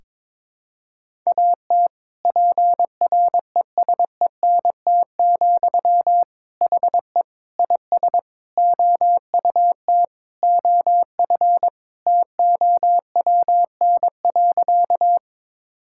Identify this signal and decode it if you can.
Morse code, heard as a keyed tone, about 21 wpm, 700 Hz. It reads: AT PRESENT, HE IS OUT OF TOWN.
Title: Sentences from Top 1000 Words - Review Part 5 - 22wpm